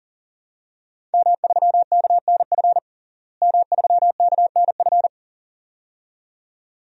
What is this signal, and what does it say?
Morse code, heard as a keyed tone, about 40 wpm, 700 Hz. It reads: M3KNF M3KNF
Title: DXCC-100 Call Signs - 5 Characters - Repeated 2x - 40wpm